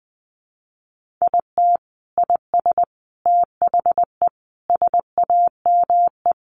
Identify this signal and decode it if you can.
Morse code, heard as a keyed tone, about 20 wpm, 700 Hz. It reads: IT IS THE SAME
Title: Sentences from Top 1100 Words 20wpm